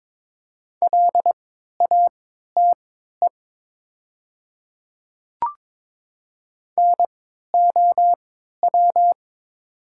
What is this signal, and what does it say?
Morse code, heard as a keyed tone, about 22 wpm, spaced as for 12 wpm, 700 Hz. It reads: LATE NOW
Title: 3x Character Spacing - Top 300 Words - 22wpm